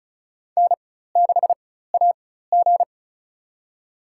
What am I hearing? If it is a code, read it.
Morse code, heard as a keyed tone, about 36 wpm, spaced as for 16 wpm, 700 Hz. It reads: N6AG
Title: USA Call Signs - 4x Character Spacing - 35wpm